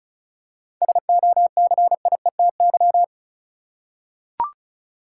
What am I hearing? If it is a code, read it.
Morse code, heard as a keyed tone, about 36 wpm, 700 Hz. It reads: SOCIETY
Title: Seven Letter Words - 35wpm